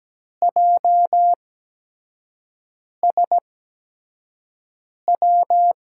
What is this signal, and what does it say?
Morse code, heard as a keyed tone, about 17 wpm, spaced as for 5 wpm, 700 Hz. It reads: JSW
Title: USA Call Signs - 5 Characters - 8x Character Spacing - 17wpm